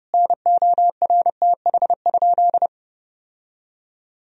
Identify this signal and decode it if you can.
Morse code, heard as a keyed tone, about 30 wpm, 700 Hz. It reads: NORTH?
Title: Sentences from Top 1000 Words - Review Part 2 - 30wpm